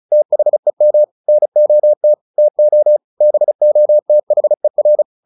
Morse code, heard as a keyed tone, about 34 wpm, 600 Hz. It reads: THEM NOT TO BOTHER
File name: the old man and the sea 2 35wpm